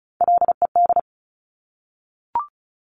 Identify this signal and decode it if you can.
Morse code, heard as a keyed tone, about 36 wpm, 700 Hz. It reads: LED
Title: Top Three Letter Combinations in Words 35wpm